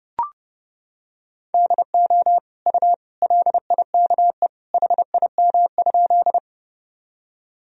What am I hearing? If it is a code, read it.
Morse code, heard as a keyed tone, about 30 wpm, 700 Hz. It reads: DO U LIKE HIM?
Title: Sentences from Top 100 Words and Abbreviations 30wpm